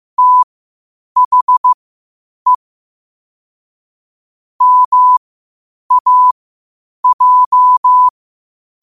Morse code, heard as a keyed tone, about 15 wpm, spaced as for 8 wpm, 1000 Hz. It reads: THE MAJ
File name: qotdfiles / 8wpm